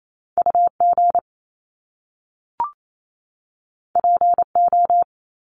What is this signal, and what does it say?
Morse code, heard as a keyed tone, about 28 wpm, 700 Hz. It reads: UG PO
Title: Top Two Letter Combinations in Words 28wpm